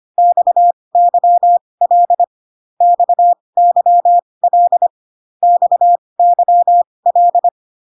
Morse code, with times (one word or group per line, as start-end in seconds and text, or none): 0.2-2.2 XYL
2.8-4.9 XYL
5.4-7.5 XYL